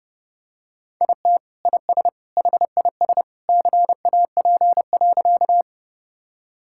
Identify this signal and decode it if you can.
Morse code, heard as a keyed tone, about 32 wpm, 700 Hz. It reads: IT IS HIS CAP.
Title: Sentences from Top 1600 Words 30wpm